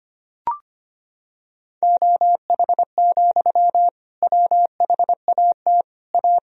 Morse code, heard as a keyed tone, about 25 wpm, 700 Hz. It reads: OH, WHAT A
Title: Sentences from Top 1000 Words - Review Part 2 - 25wpm